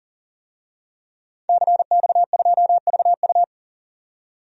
Morse code, heard as a keyed tone, about 40 wpm, 700 Hz. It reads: CX2VU
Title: DXCC-100 Call Signs - 5 Characters - 40wpm